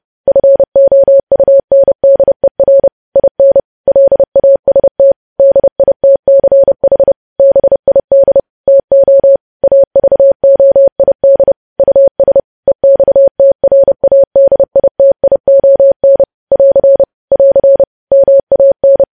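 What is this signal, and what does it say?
Morse code, heard as a keyed tone, about 30 wpm, 550 Hz. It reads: FOUNDER IN LAST DITCH BID TO AVOID US EXTRADITION <AR> <AR> MAN
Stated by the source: Morse code news headlines sent at 30.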